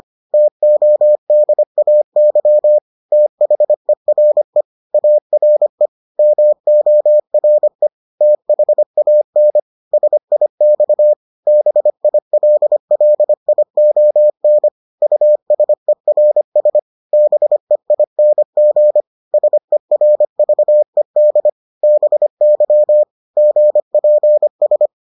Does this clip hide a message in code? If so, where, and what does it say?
Yes: Morse code from 0.3 to 24.9 s, heard as a keyed tone, about 25 wpm, 600 Hz. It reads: TODAY THERE ARE MORE THAN SIX BILLION USERS BEING SERVED BY GPS